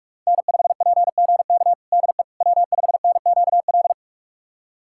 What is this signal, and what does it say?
Morse code, heard as a keyed tone, about 46 wpm, 700 Hz. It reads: N5PGK DE W5NQL
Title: USA Callsign DE Callsign - 5 Characters - 45wpm